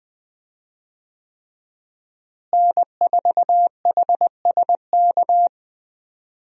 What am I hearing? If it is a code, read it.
Morse code, heard as a keyed tone, about 21 wpm, 700 Hz. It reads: N4HSK
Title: USA Call Signs - 5 Characters - Repeated 3x - 20wpm